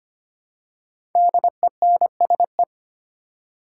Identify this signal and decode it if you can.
Morse code, heard as a keyed tone, about 25 wpm, 700 Hz. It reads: DENSE
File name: 5 Letter Words - First 3 Letters Given - 25wpm